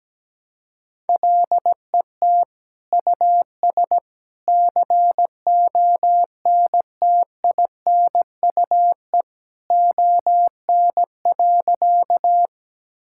Morse code, heard as a keyed tone, about 17 wpm, 700 Hz. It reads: LET US CONTINUE ON.
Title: Sentences from Top 1000 Words - Review Part 6 - 17wpm